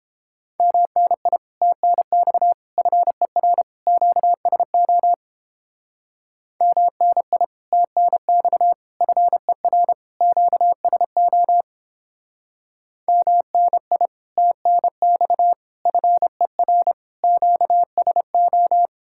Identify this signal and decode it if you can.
Morse code, heard as a keyed tone, about 31 wpm, 700 Hz. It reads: MNI TNX FER QSO MNI TNX FER QSO MNI TNX FER QSO